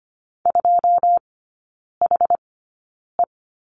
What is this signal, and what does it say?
Morse code, heard as a keyed tone, about 25 wpm, spaced as for 9 wpm, 690 Hz. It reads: 2HE
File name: Lesson 15e - Number 5 - Three Characters 25wpm